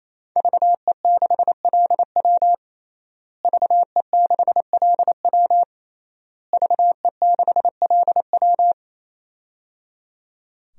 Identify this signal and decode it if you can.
Morse code, heard as a keyed tone, about 28 wpm, 700 Hz. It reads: VE6LW VE6LW VE6LW
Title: DXCC-100 Call Signs - 5 Characters - Repeated 3x - 28wpm